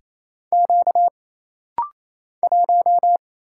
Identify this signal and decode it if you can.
Morse code, heard as a keyed tone, about 27 wpm, 700 Hz. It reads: Q 1
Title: Single Letter-Number - Rapid-Fire 28wpm